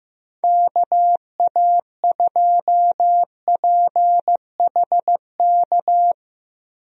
Morse code, heard as a keyed tone, about 15 wpm, 700 Hz. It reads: KA2PHK
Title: USA Call Signs - 6 Characters - 15wpm